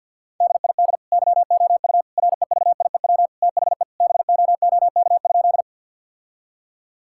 Morse code, heard as a keyed tone, about 50 wpm, 700 Hz. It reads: DID YOU REVIEW THE BOOK?
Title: Sentences from Top 1000 Words - Review Part 6 - 50wpm